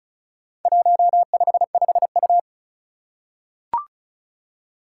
Morse code, heard as a keyed tone, about 36 wpm, 700 Hz. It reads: 155U
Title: Four Challenging Letter-Number Combinations 35wpm